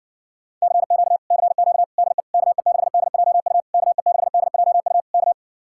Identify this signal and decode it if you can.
Morse code, heard as a keyed tone, about 60 wpm, 700 Hz. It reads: CQ CQ DE KE6DJU KE6DJU K